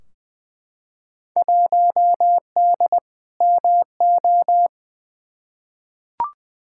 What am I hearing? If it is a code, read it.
Morse code, heard as a keyed tone, about 21 wpm, 700 Hz. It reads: 1D MO
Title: Field Day Exchanges - 20wpm